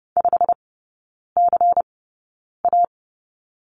Morse code, heard as a keyed tone, about 30 wpm, spaced as for 9 wpm, 700 Hz. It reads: 5CA
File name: Lesson 16e - Letter c - Three Characters 30wpm